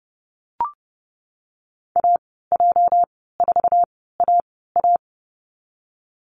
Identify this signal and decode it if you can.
Morse code, heard as a keyed tone, about 30 wpm, spaced as for 17 wpm, 700 Hz. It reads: AJ4AA
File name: Lesson 37g - Letter x - Call Signs 30wpm